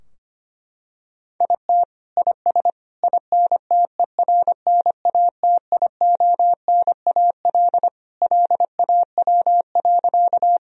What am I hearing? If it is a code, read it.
Morse code, heard as a keyed tone, about 25 wpm, 700 Hz. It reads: IT IS INTERNATIONAL LAW.
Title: Sentences from Top 1000 Words - Review Part 4 - 25wpm